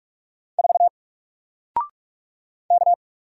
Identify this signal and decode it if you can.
Morse code, heard as a keyed tone, about 45 wpm, 700 Hz. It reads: 4 K
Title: Single Letter-Number 45wpm